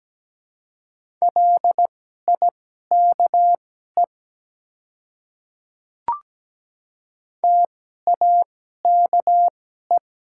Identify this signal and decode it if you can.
Morse code, heard as a keyed tone, about 17 wpm, spaced as for 12 wpm, 700 Hz. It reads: LIKE TAKE
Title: 2x Character Spacing - Top 200 Words - 17wpm